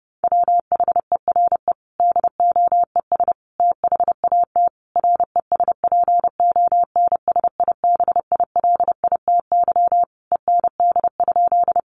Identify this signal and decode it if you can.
Morse code, heard as a keyed tone, about 30 wpm, 700 Hz. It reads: WHERE DOES THAT RESPONSIBILITY END?
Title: Sentences from Top 1200 Words 30wpm